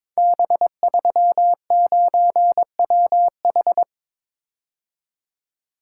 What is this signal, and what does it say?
Morse code, heard as a keyed tone, about 22 wpm, 700 Hz. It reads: B39WH
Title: Code Groups - 5 Letter:Numbers - 22wpm